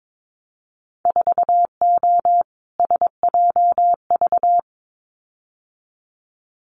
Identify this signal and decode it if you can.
Morse code, heard as a keyed tone, about 22 wpm, 700 Hz. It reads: 4O SJV
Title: Winter Field Day Exchanges Encore 22wpm